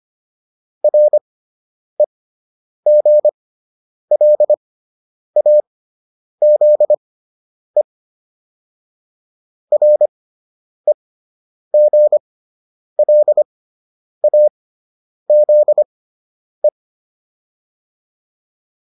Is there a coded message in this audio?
Yes: Morse code from 0.8 to 16.7 s, heard as a keyed tone, about 25 wpm, spaced as for 9 wpm, 600 Hz. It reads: REGLAZE REGLAZE